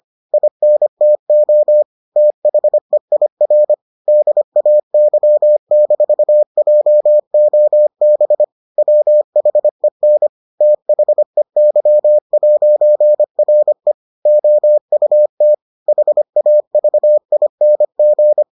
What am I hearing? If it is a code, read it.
Morse code, heard as a keyed tone, about 25 wpm, 600 Hz. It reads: INTO THEIR DAY-JOB WHEN THEY'RE OUT HAVING